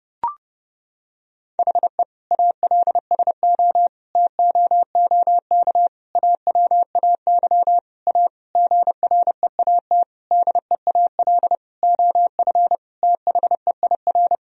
Morse code, heard as a keyed tone, about 32 wpm, 700 Hz. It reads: HE ALSO TOOK AWAY A GREAT DEAL OF THEIR
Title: Sentences from Top 1000 Words - Review Part 1 - 30wpm